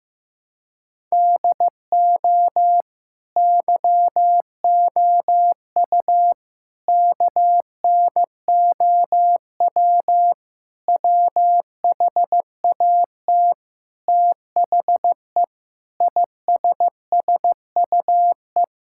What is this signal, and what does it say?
Morse code, heard as a keyed tone, about 15 wpm, 700 Hz. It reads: DO YOU KNOW WHAT THE ISSUE
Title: Sentences from Top 1000 Words - Review Part 6 - 15wpm